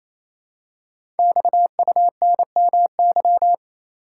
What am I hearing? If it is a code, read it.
Morse code, heard as a keyed tone, about 27 wpm, 700 Hz. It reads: XUNMY
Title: Code Groups - 5 Letters - 28wpm